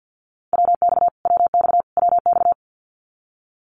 Morse code, heard as a keyed tone, about 50 wpm, 700 Hz. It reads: WX WX WX
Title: Abbreviations Repeated 3x - 50wpm